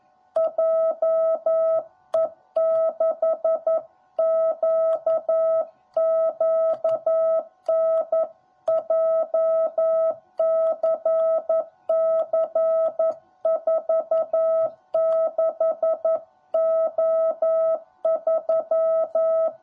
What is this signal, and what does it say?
Morse code, heard as a keyed tone, about 11 wpm, 650 Hz. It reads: JE6QQN JCC46O3
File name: jh8cbh / 2021cw / cw10b